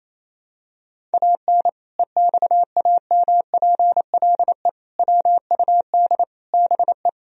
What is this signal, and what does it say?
Morse code, heard as a keyed tone, about 27 wpm, 700 Hz. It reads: AN EXAMPLE WUD BE
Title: Sentences from Top 400 Words and Abbreviations 28wpm